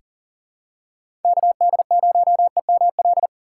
Morse code, heard as a keyed tone, about 40 wpm, 690 Hz. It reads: KD0EML
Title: USA Call Signs - 4x Character Spacing - 40wpm